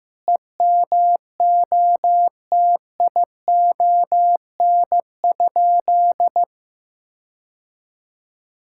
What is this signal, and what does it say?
Morse code, heard as a keyed tone, about 15 wpm, 700 Hz. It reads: EMOTION?
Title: Sentences from Top 1600 Words 15wpm